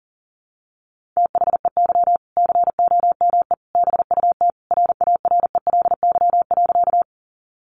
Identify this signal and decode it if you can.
Morse code, heard as a keyed tone, about 40 wpm, 700 Hz. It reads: THEY COME BUT RARELY.
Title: Sentences from Top 1400 Words 40wpm